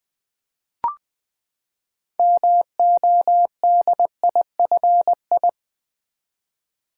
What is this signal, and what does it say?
Morse code, heard as a keyed tone, about 20 wpm, 700 Hz. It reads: MODIFI
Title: Build A Word - 6 Letters At A Time - 12-Letter Words - 20wpm